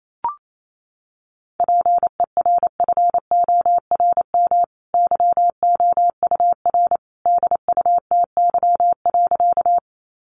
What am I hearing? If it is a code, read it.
Morse code, heard as a keyed tone, about 28 wpm, 700 Hz. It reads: PERFORM YOUR DUTY.
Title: Sentences from Top 1500 Words 28wpm